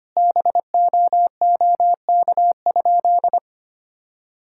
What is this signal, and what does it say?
Morse code, heard as a keyed tone, about 25 wpm, 700 Hz. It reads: BOOK?
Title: Sentences from Top 1000 Words - Review Part 5 - 25wpm